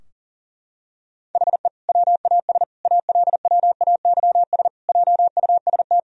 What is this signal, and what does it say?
Morse code, heard as a keyed tone, about 40 wpm, 700 Hz. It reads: HE WAS ALWAYS JUST
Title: Sentences from Top 1000 Words - Review Part 4 - 40wpm